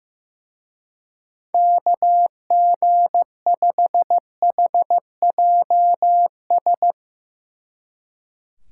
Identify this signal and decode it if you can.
Morse code, heard as a keyed tone, about 15 wpm, 700 Hz. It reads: KG5HJS
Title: Speed Racing - USA 2x3 Call Signs 15wpm